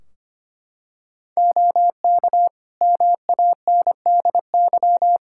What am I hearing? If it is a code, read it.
Morse code, heard as a keyed tone, about 25 wpm, 700 Hz. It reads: OK MANDY
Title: Practice QSO - Exchange 3 - More Info - 25wpm